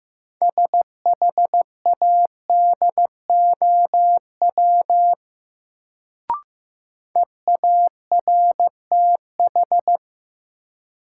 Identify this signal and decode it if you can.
Morse code, heard as a keyed tone, about 15 wpm, 700 Hz. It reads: SHADOW EARTH
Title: Sets of 5 Words Encore - New Word Drill - 15wpm